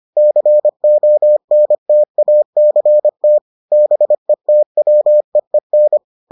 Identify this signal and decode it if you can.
Morse code, heard as a keyed tone, about 25 wpm, 600 Hz. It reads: CONTACT BETWEEN